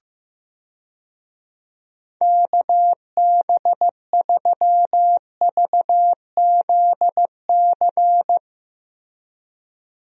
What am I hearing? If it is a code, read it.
Morse code, heard as a keyed tone, about 15 wpm, 700 Hz. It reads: KB3VZC
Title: USA Call Signs - 6 Characters - Repeated 3x - 15wpm